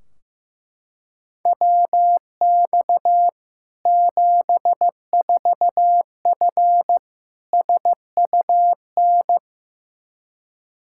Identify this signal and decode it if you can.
Morse code, heard as a keyed tone, about 16 wpm, 700 Hz. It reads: WX 74F SUN
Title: Practice QSO - Exchange 2 - Info - 15wpm